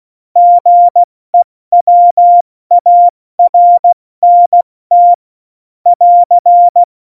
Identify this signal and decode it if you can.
Morse code, heard as a keyed tone, about 16 wpm, 700 Hz. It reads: GEWARNT <AR>
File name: Morsekurs / Witze / Witz 04. Juni 2019, 70 F80